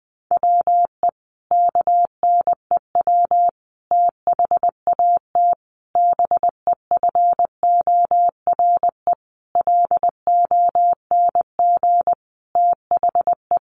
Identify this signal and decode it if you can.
Morse code, heard as a keyed tone, about 20 wpm, 700 Hz. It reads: WE KNEW THAT BEFORE LONG THE
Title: Sentences from Top 400 Words 20wpm